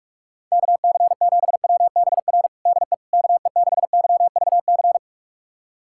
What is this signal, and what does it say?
Morse code, heard as a keyed tone, about 41 wpm, 700 Hz. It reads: KC7WBR DE KE6YVC
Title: USA Callsign DE Callsign - 6 Characters - 45wpm